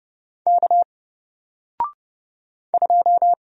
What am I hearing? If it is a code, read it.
Morse code, heard as a keyed tone, about 30 wpm, 700 Hz. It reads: K 2
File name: Single-Letter-Number-30wpm